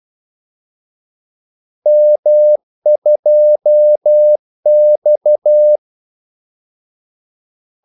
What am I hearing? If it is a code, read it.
Morse code, heard as a keyed tone, about 12 wpm, 600 Hz. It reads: M2X
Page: 2 ekstra ordmellemrum mellem kaldesignalerne